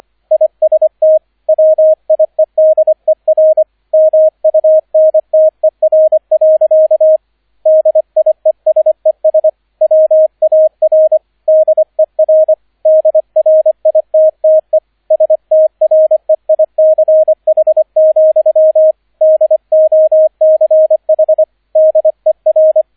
Morse code, heard as a keyed tone, about 25 wpm, 630 Hz, with radio noise. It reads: IST WIEDER MUNTER. DIESES WAR DER DRITTE STREICH, DOCH DER
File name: 003 Dritter Streich 25wpm